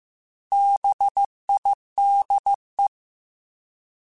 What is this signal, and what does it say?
Morse code, heard as a keyed tone, about 15 wpm, 800 Hz. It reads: BIDE
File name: Alphabetical / Alphabetical mp3s / Bs @13